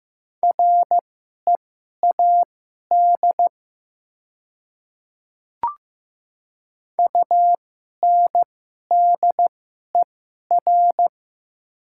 Morse code, heard as a keyed tone, about 15 wpm, spaced as for 11 wpm, 700 Hz. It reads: READ UNDER
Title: 2x Character Spacing - Top 200 Words - 15wpm